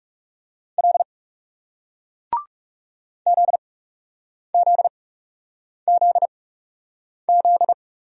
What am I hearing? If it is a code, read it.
Morse code, heard as a keyed tone, about 40 wpm, spaced as for 8 wpm, 700 Hz. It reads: L ZZZZ